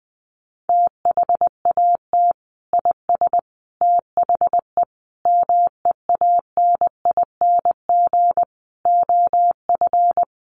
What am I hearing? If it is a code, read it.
Morse code, heard as a keyed tone, about 20 wpm, 700 Hz. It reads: THAT IS THE MEANING OF
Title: Sentences from Top 900 Words 20wpm